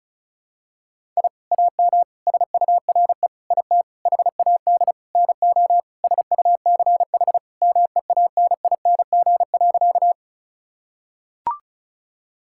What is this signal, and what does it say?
Morse code, heard as a keyed tone, about 34 wpm, 690 Hz. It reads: I AM SURE IT HAD NO SUCH MEANING.
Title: Sentences from Top 1000 Words - Review Part 6 - 35wpm